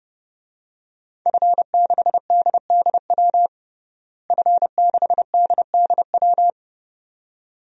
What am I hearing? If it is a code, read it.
Morse code, heard as a keyed tone, about 30 wpm, 700 Hz. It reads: F6DDW F6DDW
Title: DXCC-100 Call Signs - 5 Characters - Repeated 2x - 30wpm